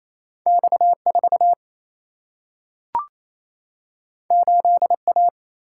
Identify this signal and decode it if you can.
Morse code, heard as a keyed tone, about 28 wpm, 700 Hz. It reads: X 4 8 A
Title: Single-Letter Single-Number 28wpm